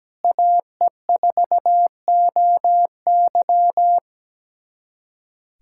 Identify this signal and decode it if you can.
Morse code, heard as a keyed tone, about 17 wpm, 700 Hz. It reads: AE4OY
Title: USA Callsign DE Callsign 17wpm